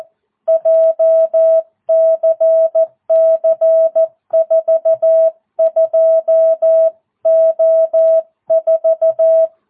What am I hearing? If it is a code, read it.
Morse code, heard as a keyed tone, about 14 wpm, 650 Hz. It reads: JCC42O4
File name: jh8cbh / 2022cw